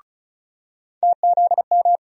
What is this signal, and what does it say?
Morse code, heard as a keyed tone, about 36 wpm, 700 Hz. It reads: TZM
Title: USA Call Sign Postfixes - 3 Letters 35wpm